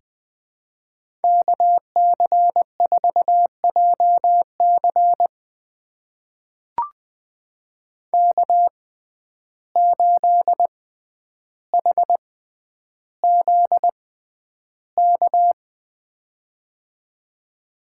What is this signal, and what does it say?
Morse code, heard as a keyed tone, about 20 wpm, spaced as for 10 wpm, 700 Hz. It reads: KC4JC K8HZK
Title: USA Call Signs - 6x Character Spacing - 20wpm